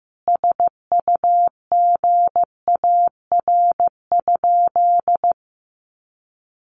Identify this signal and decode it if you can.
Morse code, heard as a keyed tone, about 15 wpm, 700 Hz. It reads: SUGAR?
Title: Sentences from Top 1200 Words 15wpm